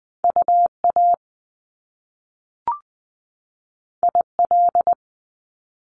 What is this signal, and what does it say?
Morse code, heard as a keyed tone, about 20 wpm, 700 Hz. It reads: UA IL
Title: Top Two Letter Combinations in Words 20wpm